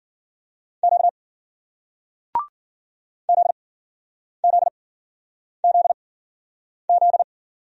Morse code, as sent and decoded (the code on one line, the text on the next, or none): --.- / --.. --.. --.. --..
Q ZZZZ